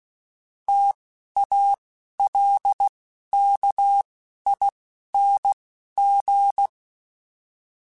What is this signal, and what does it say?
Morse code, heard as a keyed tone, about 16 wpm, spaced as for 12 wpm, 800 Hz. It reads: TALKING